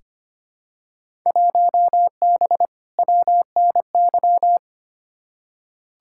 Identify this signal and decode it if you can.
Morse code, heard as a keyed tone, about 25 wpm, 700 Hz. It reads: 1B WNY
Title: Field Day Exchanges - 25wpm